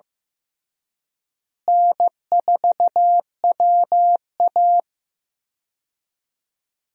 Morse code, heard as a keyed tone, about 15 wpm, 700 Hz. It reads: N4WA
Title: USA Call Signs - 4 Characters - Repeated 3x - 15wpm